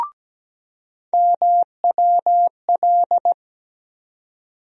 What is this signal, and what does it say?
Morse code, heard as a keyed tone, about 17 wpm, 700 Hz. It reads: MWL
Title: USA Call Sign Postfixes - 3 Letters 17wpm